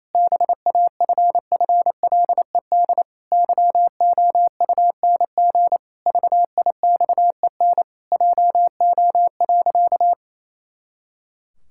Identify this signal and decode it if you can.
Morse code, heard as a keyed tone, about 28 wpm, 700 Hz. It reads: BAFFLED YOUNG VIXEN JO.
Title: Pangram Challenge v2 - 28wpm